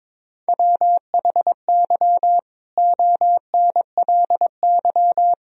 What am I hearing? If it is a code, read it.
Morse code, heard as a keyed tone, about 22 wpm, 700 Hz. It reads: WHY ONLY
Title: Sentences from Top 1000 Words - Review Part 4 - 22wpm